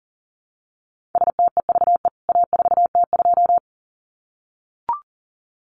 Morse code, heard as a keyed tone, about 40 wpm, 700 Hz. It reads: STEVE A4T2
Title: CWT Exchange 40wpm